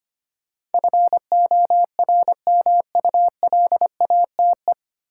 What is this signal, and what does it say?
Morse code, heard as a keyed tone, about 25 wpm, 690 Hz. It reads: FORMULATE
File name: practice / 25wpm / 9 Letter Words - First 3 Letters Given - 25wpm